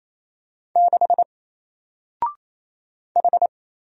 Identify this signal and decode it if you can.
Morse code, heard as a keyed tone, about 28 wpm, 700 Hz. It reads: 6 H
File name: Single-Letter-Number-28wpm